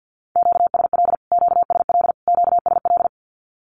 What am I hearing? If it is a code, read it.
Morse code, heard as a keyed tone, about 50 wpm, 700 Hz. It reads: QSL QSL QSL
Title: Q-Codes Repeated 3x 50wpm